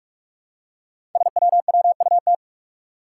Morse code, heard as a keyed tone, about 44 wpm, 700 Hz. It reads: SWWUT
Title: Five Challenging Letter-Number Combinations 45wpm